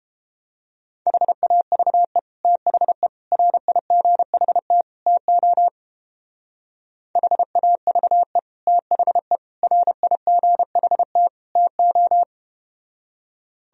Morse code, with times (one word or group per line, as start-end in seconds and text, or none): 1.1-2.2 HAVE
2.4-3.1 THE
3.3-4.8 RIGHT
5.1-5.7 TO
7.2-8.4 HAVE
8.7-9.4 THE
9.6-11.3 RIGHT
11.6-12.2 TO